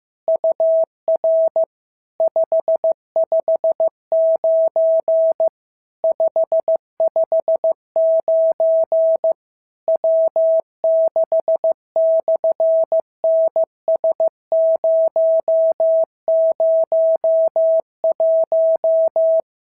Morse code, heard as a keyed tone, about 15 wpm, 650 Hz. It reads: UR 559 559 W6/NS001
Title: SOTA QSO Practice 15wpm